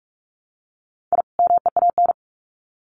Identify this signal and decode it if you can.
Morse code, heard as a keyed tone, about 44 wpm, 700 Hz. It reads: I MEAN
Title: Sets of 2 Words 45wpm